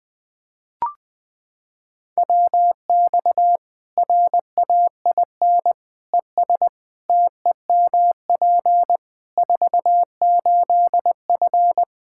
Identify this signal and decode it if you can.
Morse code, heard as a keyed tone, about 20 wpm, 700 Hz. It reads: WX RAIN ES TEMP 48F
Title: Practice QSO - Exchange 2 - Info - 20wpm